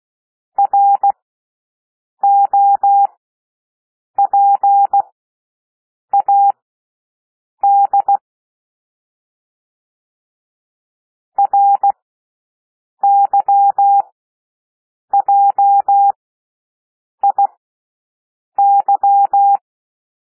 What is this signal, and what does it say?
Morse code, heard as a keyed tone, about 16 wpm, spaced as for 6 wpm, 800 Hz. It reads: ROPAD RYJIY
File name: cwmp3 / L1 SOUND LETTERS 16 wpm